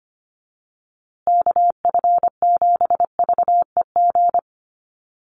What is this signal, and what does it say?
Morse code, heard as a keyed tone, about 25 wpm, 700 Hz. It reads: KF7VEG
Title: USA Call Signs Repeated 2x 25wpm